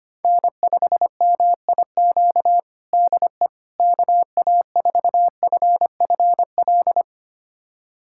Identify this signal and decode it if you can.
Morse code, heard as a keyed tone, about 25 wpm, 700 Hz. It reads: N5MIQ DE KA4FFL
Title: USA Callsign DE Callsign 25wpm